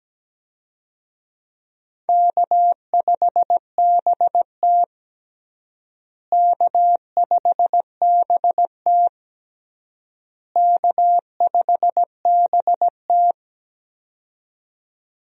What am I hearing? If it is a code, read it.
Morse code, heard as a keyed tone, about 17 wpm, 700 Hz. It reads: K5BT K5BT K5BT
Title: USA Call Signs - 4 Characters - Repeated 3x - 17wpm